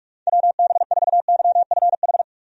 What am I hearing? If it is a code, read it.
Morse code, heard as a keyed tone, about 46 wpm, 700 Hz. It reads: WB4YFH
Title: USA Call Signs - 6 Characters - 45wpm